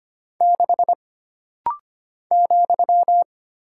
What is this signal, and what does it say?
Morse code, heard as a keyed tone, about 25 wpm, 700 Hz. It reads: 6 ,
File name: Single Character (Letter:Number:Punctuation:Prosign) - Rapid-Fire - 25wpm